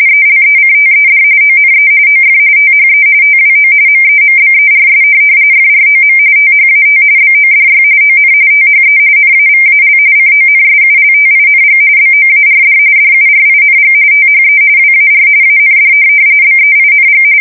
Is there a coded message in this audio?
Yes: RTTY from 0.1 to 17.2 s, about 45.45 baud. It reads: ELECTED ONE OF THE TWO MEMBERS FROM THAT BODY ON THE NAVAL CONSULTING BOARD DURING THE WAR AND BEING